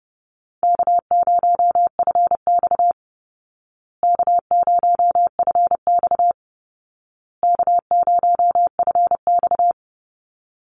Morse code, heard as a keyed tone, about 30 wpm, 700 Hz. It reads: K0FX K0FX K0FX
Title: Easy Call Signs Repeated 3x - 30wpm